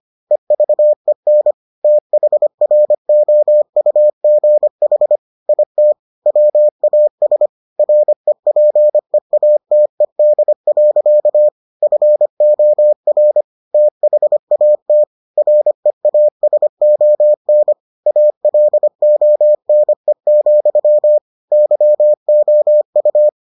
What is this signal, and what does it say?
Morse code, heard as a keyed tone, about 25 wpm, 600 Hz. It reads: EVEN THROUGH IT WAS REPEATED. FOR THAT REASON ALONE, YOU